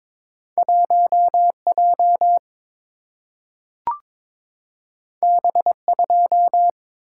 Morse code, heard as a keyed tone, about 22 wpm, 700 Hz. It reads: 1J B2
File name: practice / 22wpm / Two Letter-Number Combinations - 22wpm